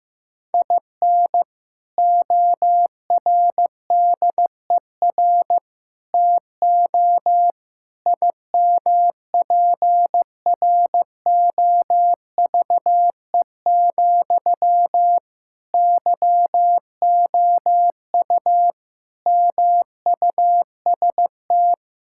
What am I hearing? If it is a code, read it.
Morse code, heard as a keyed tone, about 15 wpm, 700 Hz. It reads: IN ORDER TO IMPROVE, YOU MUST
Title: Sentences from Top 1000 Words - Review Part 5 - 15wpm